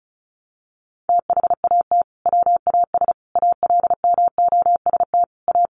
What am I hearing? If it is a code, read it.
Morse code, heard as a keyed tone, about 34 wpm, 700 Hz. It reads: THAT WAS ALMOST A
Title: Sentences from Top 1400 Words 35wpm